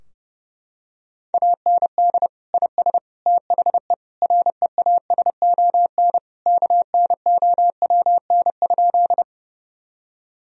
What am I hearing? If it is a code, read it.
Morse code, heard as a keyed tone, about 30 wpm, 700 Hz. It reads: AND IS THE REASON KNOWN?
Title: Sentences from Top 1000 Words - Review Part 3 - 30wpm